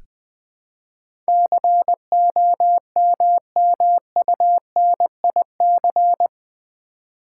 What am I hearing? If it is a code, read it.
Morse code, heard as a keyed tone, about 20 wpm, 700 Hz. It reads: COMMUNIC
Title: Build A Word - 4 Letters At A Time - 11-Letter Words - 20wpm